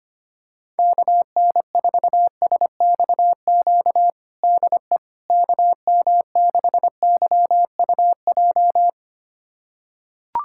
Morse code, heard as a keyed tone, about 25 wpm, 700 Hz. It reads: KN4SXQ DE KM6YUJ
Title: USA Callsign DE Callsign 25wpm